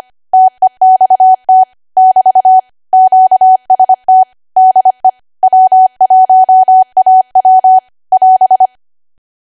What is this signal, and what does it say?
Morse code, heard as a keyed tone, about 25 wpm, 750 Hz. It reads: TEXT <BT> QST DE W1AW <AS>
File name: Archive / 25 WPM / 25WPM